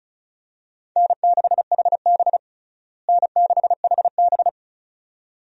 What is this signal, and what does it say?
Morse code, heard as a keyed tone, about 34 wpm, 700 Hz. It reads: N6HB N6HB
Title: USA Call Signs - 4 Characters - Repeated 2x - 35wpm